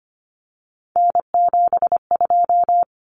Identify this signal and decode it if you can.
Morse code, heard as a keyed tone, about 25 wpm, 700 Hz. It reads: N72
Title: Common Cut Numbers - 3-Digit 25wpm